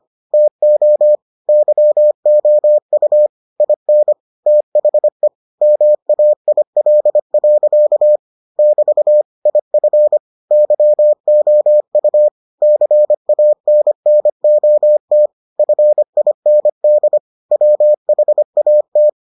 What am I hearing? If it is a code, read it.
Morse code, heard as a keyed tone, about 25 wpm, 600 Hz. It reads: TO YOU IN THE MAIL. <BT> IF YOU CANNOT FIND WHAT